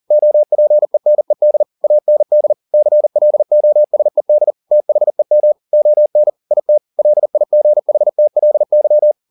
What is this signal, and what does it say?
Morse code, heard as a keyed tone, about 40 wpm, 600 Hz. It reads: OPENED AND CLOSED THEM ON IT LIGHTLY
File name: the old man and the sea 2 40wpm